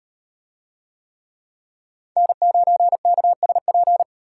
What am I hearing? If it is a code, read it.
Morse code, heard as a keyed tone, about 38 wpm, 700 Hz. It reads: N9KSP